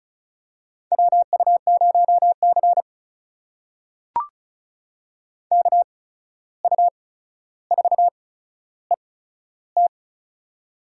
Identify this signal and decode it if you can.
Morse code, heard as a keyed tone, about 36 wpm, spaced as for 14 wpm, 700 Hz. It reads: WU0C KU4ET
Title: USA Call Signs - 8x Character Spacing - 35wpm